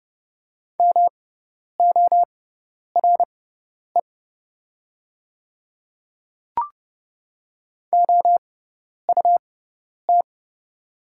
Morse code, heard as a keyed tone, about 30 wpm, spaced as for 10 wpm, 700 Hz. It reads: MORE OUT
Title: 6x Character Spacing - Top 100 Words - 30wpm